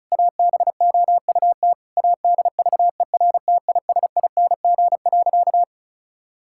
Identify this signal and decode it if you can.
Morse code, heard as a keyed tone, about 36 wpm, 700 Hz. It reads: ABOUT ADVERTISING.
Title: Sentences from Top 1000 Words - Review Part 6 - 35wpm